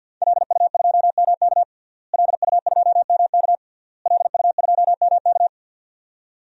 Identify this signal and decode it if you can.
Morse code, heard as a keyed tone, about 50 wpm, 700 Hz. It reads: LU2MK LU2MK LU2MK
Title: DXCC-100 Call Signs - 5 Characters - Repeated 3x - 50wpm